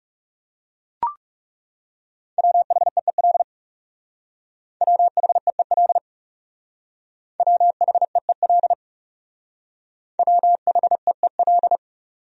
Morse code, heard as a keyed tone, about 37 wpm, 700 Hz. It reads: WHEEL WHEEL WHEEL WHEEL
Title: Speed-Racing Top 500 Words 30wpm